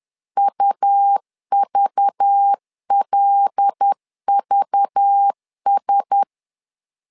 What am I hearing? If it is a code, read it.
Morse code, heard as a keyed tone, about 10 wpm, 800 Hz. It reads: UVLVS
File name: LEZIONI / 5 LBVC / 5 LBV 10wpm 2248